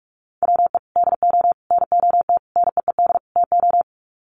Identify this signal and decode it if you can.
Morse code, heard as a keyed tone, about 46 wpm, 700 Hz. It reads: WE DO NOT NEED TO
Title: Sets of 5 Words Encore 45wpm